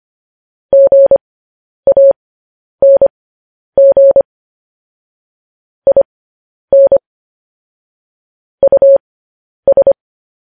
Morse code, heard as a keyed tone, about 25 wpm, spaced as for 10 wpm, 550 Hz. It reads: GANG IN US
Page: news headlines in morse code at 10 wpm